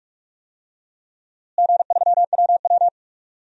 Morse code, heard as a keyed tone, about 43 wpm, 700 Hz. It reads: G3WW
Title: DXCC-100 Call Signs - 4 Characters - Repeated 3x - 45wpm